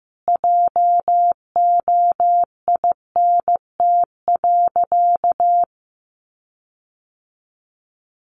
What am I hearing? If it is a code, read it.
Morse code, heard as a keyed tone, about 15 wpm, 700 Hz. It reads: JOINT.
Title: Sentences from Top 1300 Words 15wpm